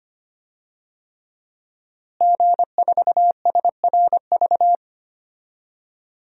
Morse code, heard as a keyed tone, about 25 wpm, 700 Hz. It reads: G4SRV
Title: DXCC-100 Call Signs - 5 Characters - Repeated 3x - 25wpm